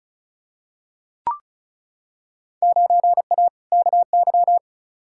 Morse code, heard as a keyed tone, about 36 wpm, 700 Hz. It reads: 9A KY
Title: Field Day Exchanges Encore - 35wpm